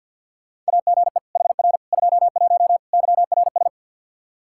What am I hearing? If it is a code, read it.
Morse code, heard as a keyed tone, about 52 wpm, 700 Hz. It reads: AGE HR 21 YRS
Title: K1USN SST QSO Practice 50wpm